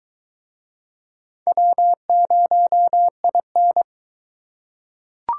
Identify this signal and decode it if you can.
Morse code, heard as a keyed tone, about 22 wpm, 700 Hz. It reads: W0IN